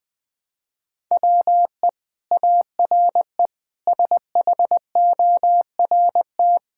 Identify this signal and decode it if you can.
Morse code, heard as a keyed tone, about 20 wpm, 700 Hz. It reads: WE ARE SHORT
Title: Sentences from Top 1000 Words - Review Part 5 - 20wpm